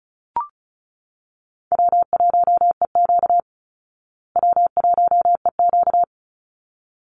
Morse code, heard as a keyed tone, about 34 wpm, 700 Hz. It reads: W1EQ W1EQ
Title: Call Signs Repeated 3x - 35wpm